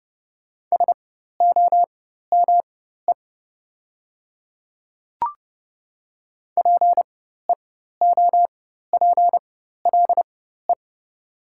Morse code, heard as a keyed tone, about 30 wpm, spaced as for 14 wpm, 700 Hz. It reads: SOME PEOPLE
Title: 4x Character Spacing - Top 200 Words - 30wpm